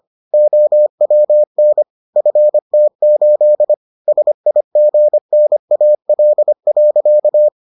Morse code, heard as a keyed tone, about 25 wpm, 600 Hz. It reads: OWN FT8 SIGNAL.